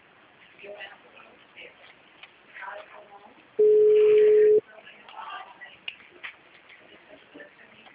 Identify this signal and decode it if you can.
Ringback tone (Europe, 425 Hz), with speech.